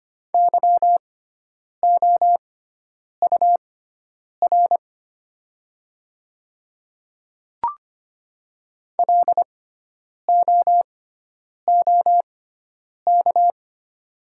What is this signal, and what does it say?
Morse code, heard as a keyed tone, about 25 wpm, spaced as for 9 wpm, 700 Hz. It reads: YOUR LOOK
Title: 6x Character Spacing - Top 100 Words - 25wpm